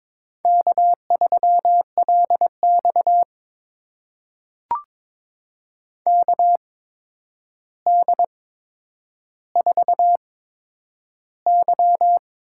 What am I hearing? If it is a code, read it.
Morse code, heard as a keyed tone, about 22 wpm, spaced as for 9 wpm, 700 Hz. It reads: K3LX KD4Y
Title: USA Call Signs - 8x Character Spacing - 22wpm